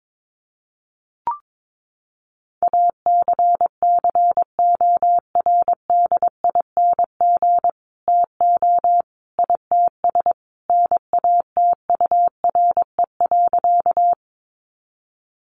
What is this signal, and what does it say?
Morse code, heard as a keyed tone, about 22 wpm, 700 Hz. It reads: ACCORDING TO ITS NATURE.
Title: Sentences from Top 700 Words 22wpm